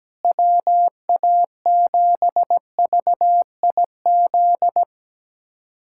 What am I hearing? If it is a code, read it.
Morse code, heard as a keyed tone, about 17 wpm, 700 Hz. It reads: WA7VIZ
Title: USA Callsign DE Callsign - 6 Characters - 17wpm